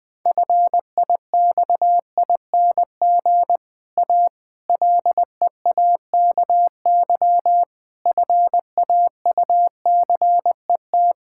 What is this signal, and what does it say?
Morse code, heard as a keyed tone, about 20 wpm, 700 Hz. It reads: FIXING A LEAKY FAUCET
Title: Difficult Sentences with Hints - 20wpm